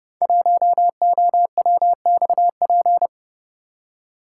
Morse code, heard as a keyed tone, about 30 wpm, 700 Hz. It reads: 1OWXP
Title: Code Groups - 5 Letter:Numbers - 30wpm